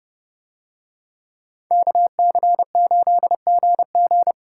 Morse code, heard as a keyed tone, about 29 wpm, 700 Hz. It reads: KC8GG